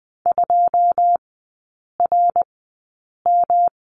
Morse code, heard as a keyed tone, about 20 wpm, spaced as for 8 wpm, 700 Hz. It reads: 2RM
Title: 121 Lesson 20e - Number 3 - Three Characters 20wpm